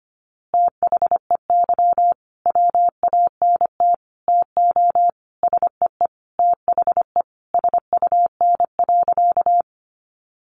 Morse code, heard as a keyed tone, about 25 wpm, 700 Hz. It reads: THEY WANT TO SEE THE SUN.
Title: Easy Phrases 25wpm